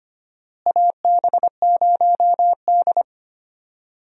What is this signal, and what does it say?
Morse code, heard as a keyed tone, about 25 wpm, 700 Hz. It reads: AB0D
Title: USA Call Signs - 4 Characters - Repeated 2x - 25wpm